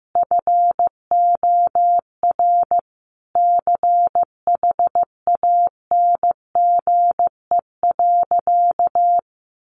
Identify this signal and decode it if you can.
Morse code, heard as a keyed tone, about 15 wpm, 700 Hz. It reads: FOR CHANGE.
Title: Sentences from Top 1500 Words 15wpm